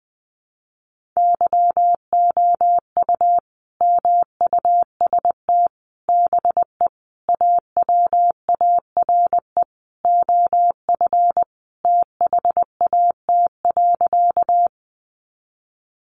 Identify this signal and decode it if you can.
Morse code, heard as a keyed tone, about 20 wpm, 700 Hz. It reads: YOU MUST BE AWARE OF THAT.
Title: Sentences from Top 1200 Words 20wpm